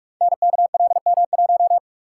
Morse code, heard as a keyed tone, about 44 wpm, 700 Hz. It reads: NKLM1
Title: Code Groups - 5 Letter:Numbers - 45wpm